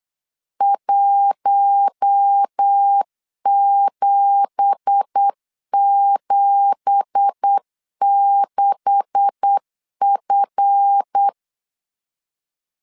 Morse code, heard as a keyed tone, about 8 wpm, 800 Hz. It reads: 1776F
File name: 8wpm